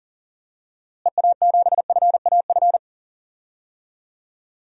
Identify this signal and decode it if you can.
Morse code, heard as a keyed tone, about 40 wpm, 700 Hz. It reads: EA7FAF
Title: DXCC-100 Call Signs - 6 Characters - Repeated 3x - 40wpm